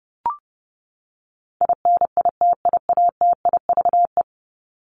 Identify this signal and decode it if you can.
Morse code, heard as a keyed tone, about 30 wpm, 700 Hz. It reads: INITIATIVE
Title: Top 1751-2000 Words 30wpm